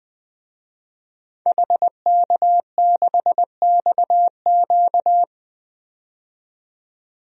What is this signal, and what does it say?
Morse code, heard as a keyed tone, about 20 wpm, 700 Hz. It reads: HK6XQ
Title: DXCC-100 Call Signs - 5 Characters - 20wpm